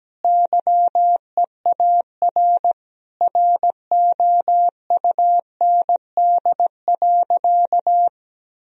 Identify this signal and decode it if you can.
Morse code, heard as a keyed tone, about 17 wpm, 700 Hz. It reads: YEAR ROUND.
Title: Sentences from Top 1000 Words - Review Part 1 - 17wpm